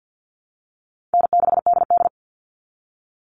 Morse code, heard as a keyed tone, about 48 wpm, 700 Hz. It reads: N6DD
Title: All Cut Numbers - 4-Digit 50wpm